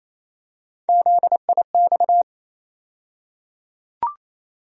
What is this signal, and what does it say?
Morse code, heard as a keyed tone, about 28 wpm, 700 Hz. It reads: ZIX
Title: USA Call Sign Postfixes - 3 Letters 28wpm